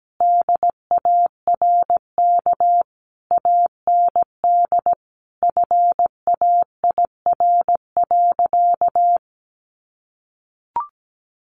Morse code, heard as a keyed tone, about 17 wpm, 700 Hz. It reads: DARK AND FAIR.
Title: Sentences from Top 1300 Words 17wpm